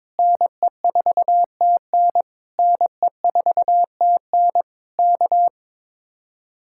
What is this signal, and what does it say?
Morse code, heard as a keyed tone, about 21 wpm, 700 Hz. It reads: NE4TN NE4TN K
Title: Speed-Racing Common QSO Phrases - 22wpm